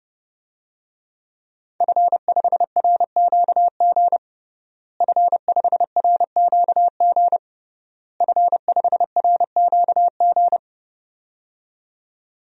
Morse code, heard as a keyed tone, about 30 wpm, 700 Hz. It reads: F5RQG F5RQG F5RQG
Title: DXCC-100 Call Signs - 5 Characters - Repeated 3x - 30wpm